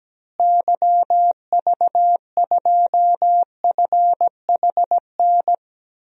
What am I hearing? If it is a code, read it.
Morse code, heard as a keyed tone, about 17 wpm, 700 Hz. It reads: YV2FHN
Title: DXCC-100 Call Signs - 6 Characters - Repeated 3x - 17wpm